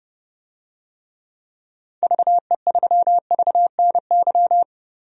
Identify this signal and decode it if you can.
Morse code, heard as a keyed tone, about 32 wpm, 700 Hz. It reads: VE3VNY
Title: DXCC-100 Call Signs - 6 Characters - Repeated 3x - 30wpm